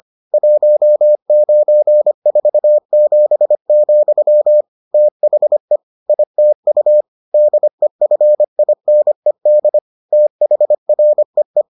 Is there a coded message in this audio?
Yes: Morse code from 0.3 to 11.6 s, heard as a keyed tone, about 25 wpm, 600 Hz. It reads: 1947, THE ITU DEFINED THREE